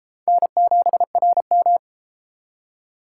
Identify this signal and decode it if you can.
Morse code, heard as a keyed tone, about 29 wpm, 700 Hz. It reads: N7RM